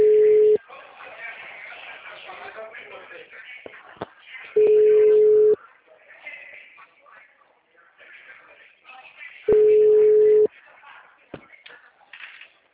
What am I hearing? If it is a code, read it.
Ringback tone (Europe, 425 Hz), with speech.
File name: out- -2013- - - .14789